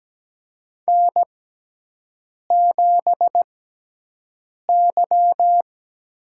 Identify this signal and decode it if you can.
Morse code, heard as a keyed tone, about 17 wpm, spaced as for 6 wpm, 700 Hz. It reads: N7Y
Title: USA Call Signs - 6x Character Spacing - 17wpm